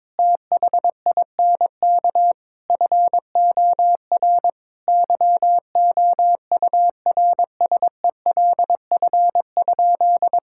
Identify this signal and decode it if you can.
Morse code, heard as a keyed tone, about 22 wpm, 700 Hz. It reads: THINK FOR YOURSELF?
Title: Sentences from Top 1000 Words - Review Part 6 - 22wpm